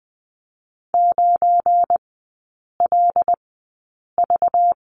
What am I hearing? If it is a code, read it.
Morse code, heard as a keyed tone, about 20 wpm, spaced as for 8 wpm, 700 Hz. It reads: 9LV
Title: 218 Lesson 35e - Number 0 - Three Characters 20wpm